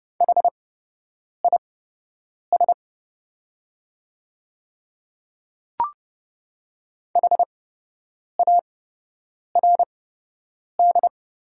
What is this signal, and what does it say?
Morse code, heard as a keyed tone, about 30 wpm, spaced as for 8 wpm, 690 Hz. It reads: HIS HARD
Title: 8x Character Spacing - Top 500 Words - 30wpm